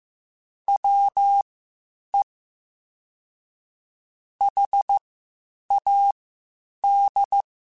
Morse code, heard as a keyed tone, about 15 wpm, spaced as for 8 wpm, 800 Hz. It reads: WE HAD
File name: Spaced Characters / Spaced Characters-mp3s / Weather 4 24 sp @13